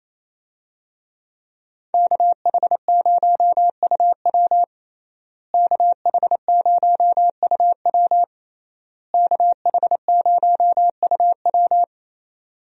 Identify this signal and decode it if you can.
Morse code, heard as a keyed tone, about 28 wpm, 700 Hz. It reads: KH0UW KH0UW KH0UW
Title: USA Call Signs - 5 Characters - Repeated 3x - 28wpm